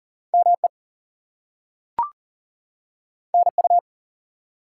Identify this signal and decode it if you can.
Morse code, heard as a keyed tone, about 40 wpm, 700 Hz. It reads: ME NU
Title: US States and Canadian Provinces 40wpm